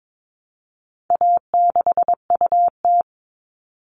Morse code, heard as a keyed tone, about 22 wpm, 700 Hz. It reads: A6UT
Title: All Cut Numbers - 4-Digit 22wpm